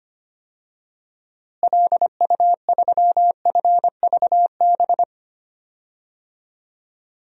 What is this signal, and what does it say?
Morse code, heard as a keyed tone, about 25 wpm, 700 Hz. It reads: LU3FVB
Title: DXCC-100 Call Signs - 6 Characters - Repeated 3x - 25wpm